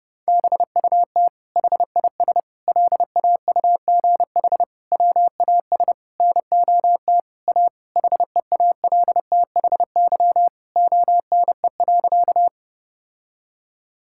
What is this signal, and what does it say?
Morse code, heard as a keyed tone, about 30 wpm, 700 Hz. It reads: BUT HIS LAUGH WAS NOT A HEALTHY ONE.
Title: Sentences from Top 1000 Words - Review Part 5 - 30wpm